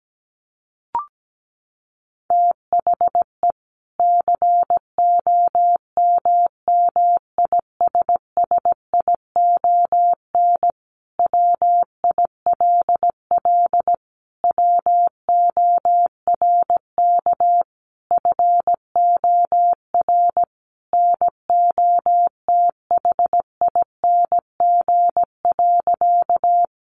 Morse code, heard as a keyed tone, about 17 wpm, 700 Hz. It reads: THE COMMISSION WILL WORK FOR NOTHING.
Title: Sentences from Top 1400 Words 17wpm